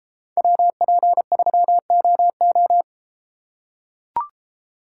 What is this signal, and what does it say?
Morse code, heard as a keyed tone, about 33 wpm, 700 Hz. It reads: WP3OO